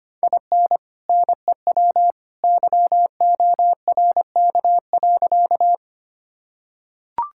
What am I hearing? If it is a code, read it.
Morse code, heard as a keyed tone, about 25 wpm, 700 Hz. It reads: IN NEW YORK.
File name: practice / 25wpm / Sentences with 2-Word Phrases - 25wpm